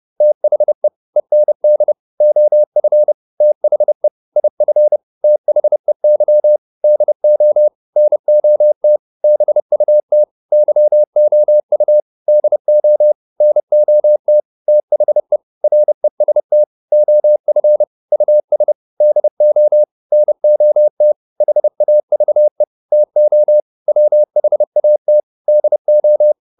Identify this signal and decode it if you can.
Morse code, heard as a keyed tone, about 30 wpm, 600 Hz. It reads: THE END OF THE IF THEY DO NOT BUT YOU DO NOT THE REST OF US DO NOT HAVE TO WHAT DO